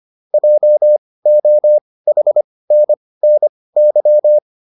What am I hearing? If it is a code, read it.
Morse code, heard as a keyed tone, about 25 wpm, spaced as for 18 wpm, 600 Hz. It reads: JOHNNY
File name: QSO306 18